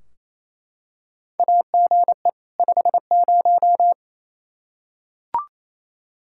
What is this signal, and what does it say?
Morse code, heard as a keyed tone, about 27 wpm, 700 Hz. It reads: AGE 50
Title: Practice QSO - Exchange 3 - More Info - 28wpm